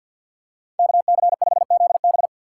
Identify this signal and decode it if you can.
Morse code, heard as a keyed tone, about 48 wpm, 700 Hz. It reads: KC5ZB
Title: USA Call Signs - 5 Characters - 2x Character Spacing - 50wpm